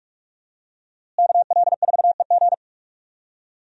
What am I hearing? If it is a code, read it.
Morse code, heard as a keyed tone, about 42 wpm, 700 Hz. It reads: KL4EG
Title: USA Call Signs - 5 Characters - Repeated 2x - 45wpm